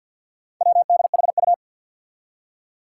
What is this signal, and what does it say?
Morse code, heard as a keyed tone, about 52 wpm, 700 Hz. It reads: WDHU
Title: Four Challenging Letter-Number Combinations 50wpm